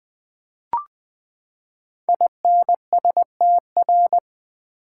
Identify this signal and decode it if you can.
Morse code, heard as a keyed tone, about 20 wpm, 700 Hz. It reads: INSTR
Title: Build A Word - 5 Letters At A Time - 12-Letter Words - 20wpm